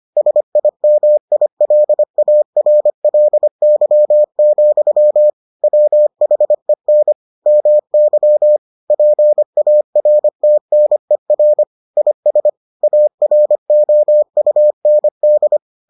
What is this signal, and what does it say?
Morse code, heard as a keyed tone, about 25 wpm, 600 Hz. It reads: SIMILARLY, WHEN MY PARTNER IS AROUND